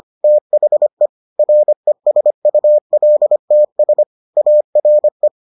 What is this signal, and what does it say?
Morse code, heard as a keyed tone, about 25 wpm, 600 Hz. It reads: THE RESULTS ARE